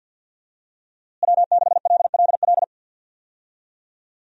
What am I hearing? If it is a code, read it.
Morse code, heard as a keyed tone, about 50 wpm, 700 Hz. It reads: W6LLL
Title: USA Call Signs - 5 Characters - Repeated 3x - 50wpm